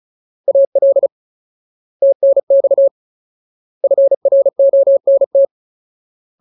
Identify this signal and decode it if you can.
Morse code, heard as a keyed tone, about 34 wpm, 550 Hz. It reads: AL TNX FRONT
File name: Ultimate Word List / 35WPM Ultimate